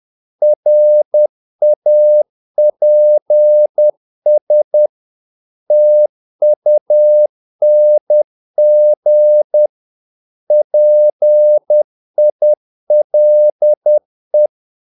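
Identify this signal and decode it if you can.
Morse code, heard as a keyed tone, about 10 wpm, 600 Hz. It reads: RAPS TUNG PILE
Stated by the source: Morse practice of short Danish words at 10 wpm, from CW med Gnister.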